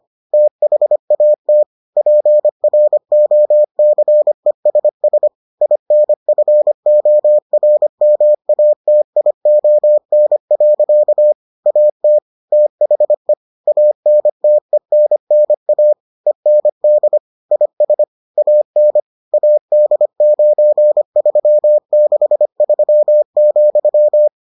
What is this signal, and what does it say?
Morse code, heard as a keyed tone, about 25 wpm, 600 Hz. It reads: THAT PROCESS INFORMATION. AT THE ANTENNA END IS AN AD9363,